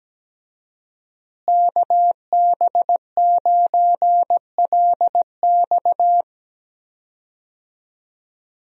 Morse code, heard as a keyed tone, about 17 wpm, 700 Hz. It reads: KB9LX
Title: USA Call Signs - 5 Characters - Repeated 2x - 17wpm